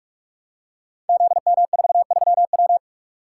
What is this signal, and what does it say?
Morse code, heard as a keyed tone, about 44 wpm, 700 Hz. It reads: ZM43W
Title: Code Groups - 5 Letter:Numbers - 45wpm